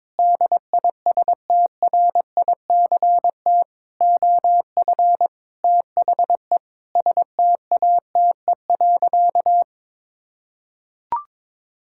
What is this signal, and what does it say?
Morse code, heard as a keyed tone, about 22 wpm, 700 Hz. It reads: DISTRICT OF THE STATE.
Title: Sentences from Top 1600 Words 22wpm